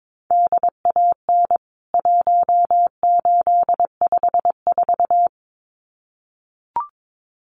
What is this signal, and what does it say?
Morse code, heard as a keyed tone, about 22 wpm, 700 Hz. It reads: DAN 1854
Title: CWT Exchange 22wpm